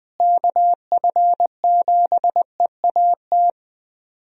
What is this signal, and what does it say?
Morse code, heard as a keyed tone, about 20 wpm, 700 Hz. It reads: KF7EAT
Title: USA Call Signs - 6 Characters - Repeated 3x - 20wpm